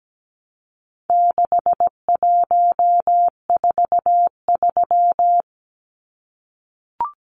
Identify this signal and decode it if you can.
Morse code, heard as a keyed tone, about 17 wpm, 700 Hz. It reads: 6143
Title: Numbers - 4-Digit 17wpm